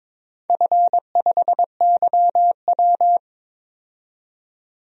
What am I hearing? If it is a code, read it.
Morse code, heard as a keyed tone, about 22 wpm, 700 Hz. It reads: F5YW
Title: DXCC-100 Call Signs - 4 Characters - 22wpm